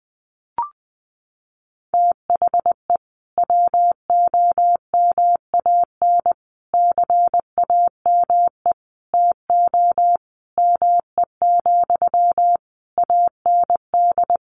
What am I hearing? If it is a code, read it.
Morse code, heard as a keyed tone, about 19 wpm, 700 Hz. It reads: THE WOMAN CAME TO ME, AND
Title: Sentences from Top 1100 Words 20wpm